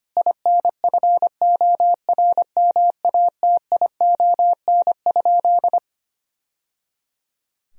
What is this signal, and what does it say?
Morse code, heard as a keyed tone, about 25 wpm, 700 Hz. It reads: INFORMATION?
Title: Sentences from Top 1000 Words - Review Part 3 - 25wpm